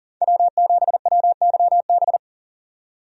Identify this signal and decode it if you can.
Morse code, heard as a keyed tone, about 40 wpm, 700 Hz. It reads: W7WYB
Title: USA Call Signs - 5 Characters - 40wpm